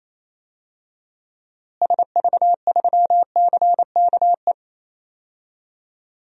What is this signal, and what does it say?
Morse code, heard as a keyed tone, about 28 wpm, 700 Hz. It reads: SV3CKE
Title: DXCC-100 Call Signs - 6 Characters - Repeated 3x - 28wpm